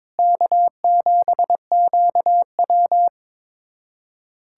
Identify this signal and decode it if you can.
Morse code, heard as a keyed tone, about 22 wpm, 700 Hz. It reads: K7QW
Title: USA Callsign DE Callsign - 4 Characters - 22wpm